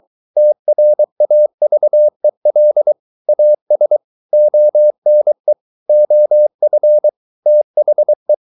Morse code, heard as a keyed tone, about 23 wpm, 600 Hz. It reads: TRAVEL AS ONE OF THE